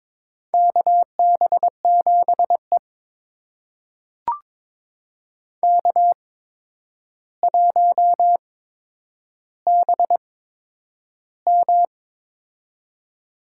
Morse code, heard as a keyed tone, about 22 wpm, spaced as for 9 wpm, 700 Hz. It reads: KB7E K1BM
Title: USA Call Signs - 8x Character Spacing - 22wpm